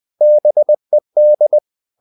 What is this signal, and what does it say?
Morse code, heard as a keyed tone, about 20 wpm, 600 Hz. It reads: BED
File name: the old man and the sea / the old man and the sea part 1 20wpm